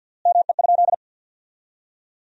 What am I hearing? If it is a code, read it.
Morse code, heard as a keyed tone, about 50 wpm, 700 Hz. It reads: ME?
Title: 3x Word-Spacing - Sentences from Top 200 Words - 50wpm